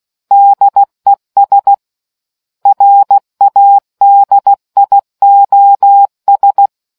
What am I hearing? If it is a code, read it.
Morse code, heard as a keyed tone, about 16 wpm, 800 Hz. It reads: DES RADIOS